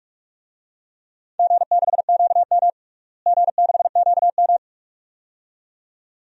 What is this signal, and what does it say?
Morse code, heard as a keyed tone, about 42 wpm, 700 Hz. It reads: G6QM G6QM
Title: DXCC-100 Call Signs - 4 Characters - Repeated 2x - 45wpm